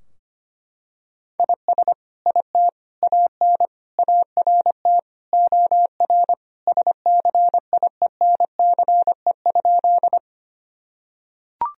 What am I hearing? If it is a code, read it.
Morse code, heard as a keyed tone, about 25 wpm, 700 Hz. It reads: IS IT AN ART OR SCIENCE?
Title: Sentences from Top 1000 Words - Review Part 3 - 25wpm